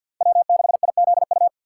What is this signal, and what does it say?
Morse code, heard as a keyed tone, about 47 wpm, 700 Hz. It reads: W6IZU
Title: USA Call Signs - 5 Characters - 3x Character Spacing - 50wpm